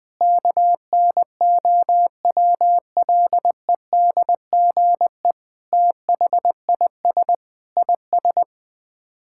Morse code, heard as a keyed tone, about 20 wpm, 700 Hz. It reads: KNOWLEDGE THIS IS
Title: Sets of 8 Words - 20wpm